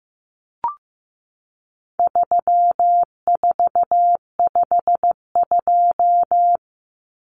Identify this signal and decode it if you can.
Morse code, heard as a keyed tone, about 15 wpm, 700 Hz. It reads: 3452
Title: Numbers - 4-Digit 15wpm